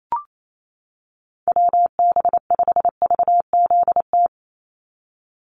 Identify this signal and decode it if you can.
Morse code, heard as a keyed tone, about 28 wpm, 700 Hz. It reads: WB5VZT
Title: USA Call Signs 28wpm